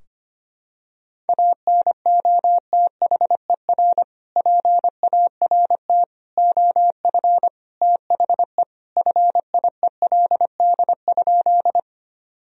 Morse code, heard as a keyed tone, about 25 wpm, 700 Hz. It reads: ANOTHER PART OF THE FIELD?
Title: Sentences from Top 1000 Words - Review Part 2 - 25wpm